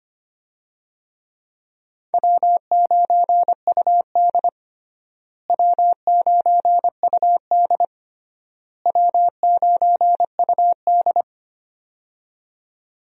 Morse code, heard as a keyed tone, about 25 wpm, 700 Hz. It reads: W9UD W9UD W9UD
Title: USA Call Signs - 4 Characters - Repeated 3x - 25wpm